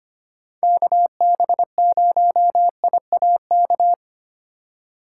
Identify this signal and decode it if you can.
Morse code, heard as a keyed tone, about 25 wpm, 700 Hz. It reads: KB0IAK
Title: USA Call Signs - 6 Characters - 2x Character Spacing - 25wpm